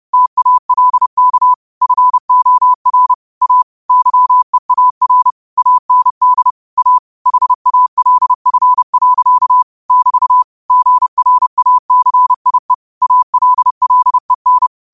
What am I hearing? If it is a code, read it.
Morse code, heard as a keyed tone, about 30 wpm, 1000 Hz. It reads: TALK FOR A YEAR AND A HALF. <BT> GRACIE ALLEN
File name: qotdfiles / 30wpm